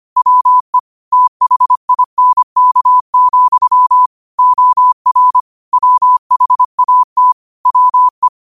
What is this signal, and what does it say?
Morse code, heard as a keyed tone, about 25 wpm, 1000 Hz. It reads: WE THINK, OR WHAT WE
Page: New quotes every day in morse code at 25 Words per minute.